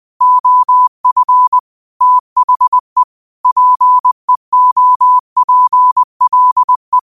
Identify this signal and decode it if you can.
Morse code, heard as a keyed tone, about 20 wpm, 1000 Hz. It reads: OF THE PEOPLE
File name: qotdfiles / 20wpm